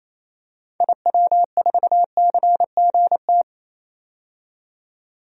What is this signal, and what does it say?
Morse code, heard as a keyed tone, about 28 wpm, 700 Hz. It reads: IW4CGT
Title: DXCC-100 Call Signs - 6 Characters - 28wpm